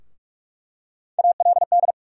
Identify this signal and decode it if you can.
Morse code, heard as a keyed tone, about 42 wpm, 700 Hz. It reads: ALD
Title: USA Call Sign Postfixes - 3 Letters 45wpm